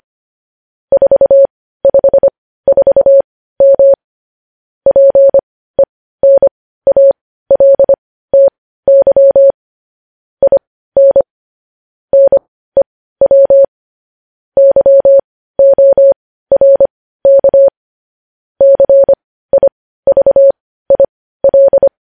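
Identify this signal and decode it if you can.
Morse code, heard as a keyed tone, about 25 wpm, spaced as for 15 wpm, 550 Hz. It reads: 454M PENALTY IN NEW YORK CIVIL
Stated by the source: Morse code news headlines sent at 15 wpm.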